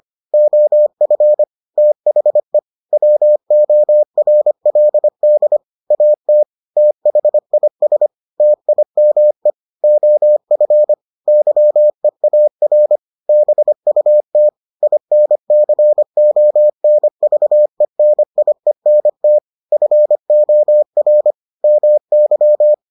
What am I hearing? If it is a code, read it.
Morse code, heard as a keyed tone, about 25 wpm, 600 Hz. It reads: OF THE WORLD AT THIS TIME OF YEAR BUT INCONVENIENT FOR MY